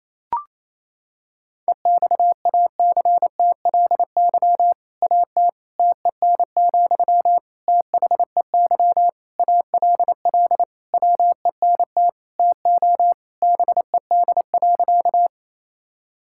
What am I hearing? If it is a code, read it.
Morse code, heard as a keyed tone, about 28 wpm, 700 Hz. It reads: EXACTLY AT TEN, THEY ALL WENT TO BED.
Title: Sentences from Top 1000 Words - Review Part 6 - 28wpm